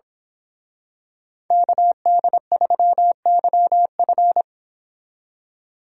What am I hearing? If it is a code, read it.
Morse code, heard as a keyed tone, about 26 wpm, 700 Hz. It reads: KD3YF
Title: Speed Racing - USA 2x2 Call Signs 22wpm